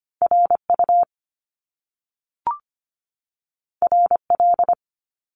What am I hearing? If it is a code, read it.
Morse code, heard as a keyed tone, about 25 wpm, 700 Hz. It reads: RU RL
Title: Top Two Letter Combinations in Words 25wpm